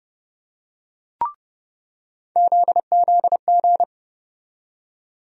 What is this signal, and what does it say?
Morse code, heard as a keyed tone, about 30 wpm, 700 Hz. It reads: ZZG
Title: USA Call Sign Postfixes - 3 Letters 30wpm